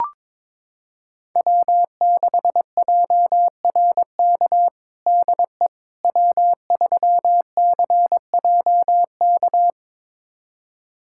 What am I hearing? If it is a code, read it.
Morse code, heard as a keyed tone, about 22 wpm, 700 Hz. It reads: W6JRK DE W3CJK
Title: USA Callsign DE Callsign - 5 Characters - 22wpm